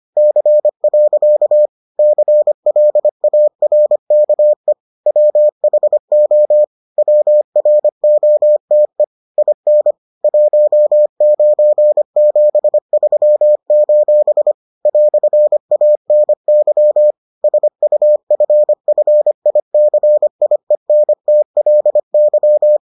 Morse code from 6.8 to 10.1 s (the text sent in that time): WROTE IN